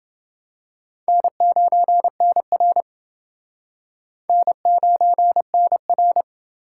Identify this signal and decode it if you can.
Morse code, heard as a keyed tone, about 30 wpm, 700 Hz. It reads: N9NR N9NR